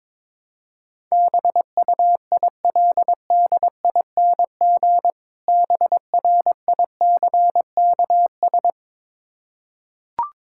Morse code, heard as a keyed tone, about 22 wpm, 690 Hz. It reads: BUILDING BRICKS
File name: Difficult Words with Hints - 22wpm